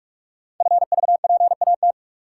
Morse code, heard as a keyed tone, about 44 wpm, 700 Hz. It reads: FVPAT
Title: Code Groups - 5 Letters - 45wpm